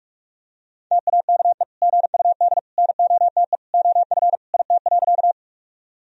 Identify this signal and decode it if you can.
Morse code, heard as a keyed tone, about 46 wpm, 700 Hz. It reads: TAKE GUD NOTE OF IT.
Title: Sentences from Top 500 Words and Abbreviations 45wpm